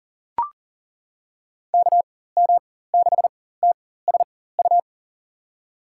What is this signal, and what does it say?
Morse code, heard as a keyed tone, about 40 wpm, spaced as for 19 wpm, 700 Hz. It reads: KM6TSU
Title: USA Call Signs - 6 Characters - 4x Character Spacing - 40wpm